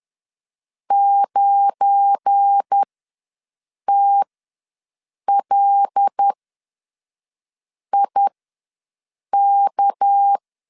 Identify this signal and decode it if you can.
Morse code, heard as a keyed tone, about 10 wpm, spaced as for 5 wpm, 800 Hz. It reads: 9TLIK
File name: LEZIONI / 7 GF9Z / 7 GF9 ws 10wpm 4034